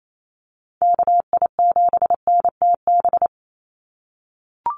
Morse code, heard as a keyed tone, about 28 wpm, 700 Hz. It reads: KI7NTB
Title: USA Call Signs 28wpm